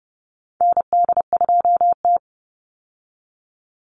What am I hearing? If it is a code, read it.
Morse code, heard as a keyed tone, about 28 wpm, 700 Hz. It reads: ND2T
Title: Call Signs Repeated 2x - 30wpm